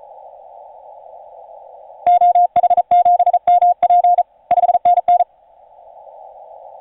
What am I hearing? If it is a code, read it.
Morse code, heard as a keyed tone, about 35 wpm, 710 Hz, with radio noise. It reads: OH7MP 5NN
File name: 5u5r 17m cw